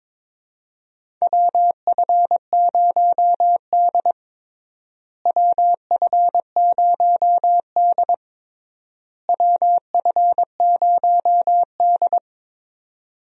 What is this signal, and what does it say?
Morse code, heard as a keyed tone, about 22 wpm, 700 Hz. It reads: WF0D WF0D WF0D
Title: USA Call Signs - 4 Characters - Repeated 3x - 22wpm